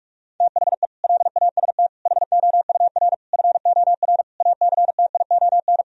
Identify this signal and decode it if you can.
Morse code, heard as a keyed tone, about 41 wpm, 700 Hz. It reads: THE LAST HOUR FOR ACTION
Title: Sentences from Top 1000 Words - Review Part 4 - 45wpm